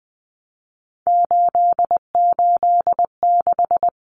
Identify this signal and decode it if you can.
Morse code, heard as a keyed tone, about 20 wpm, 700 Hz. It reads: 886
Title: Numbers - 3-Digit 20wpm